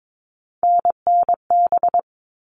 Morse code, heard as a keyed tone, about 22 wpm, 700 Hz. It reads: NNB
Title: All Cut Numbers - 3-Digit 22wpm